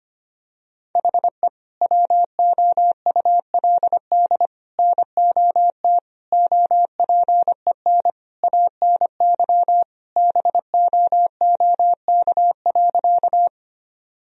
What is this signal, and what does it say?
Morse code, heard as a keyed tone, about 25 wpm, 700 Hz. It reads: HE WOULD NOT OPEN ANY BOOK.
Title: Sentences from Top 1000 Words - Review Part 1 - 25wpm